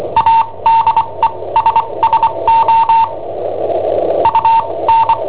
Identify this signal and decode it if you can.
Morse code, heard as a keyed tone, about 24 wpm, spaced as for 19 wpm, 910 Hz, with noise. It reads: ADESSO UN
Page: Cassetta esercizi preparata da I7OHP per IW5ELP